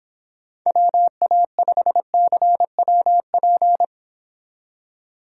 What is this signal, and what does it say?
Morse code, heard as a keyed tone, about 26 wpm, 700 Hz. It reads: WA5CWP